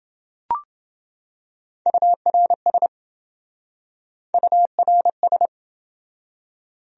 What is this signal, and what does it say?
Morse code, heard as a keyed tone, about 27 wpm, 700 Hz. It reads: URS URS